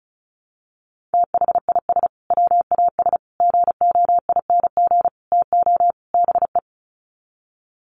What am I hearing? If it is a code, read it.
Morse code, heard as a keyed tone, about 34 wpm, 700 Hz. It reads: THIS WAS GOING TO BE
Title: Sets of 5 Words Encore2 35wpm